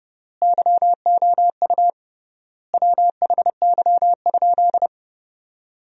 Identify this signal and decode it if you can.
Morse code, heard as a keyed tone, about 30 wpm, 700 Hz. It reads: YOU WHY?
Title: 3x Word-Spacing - Sentences from Top 200 Words - 30wpm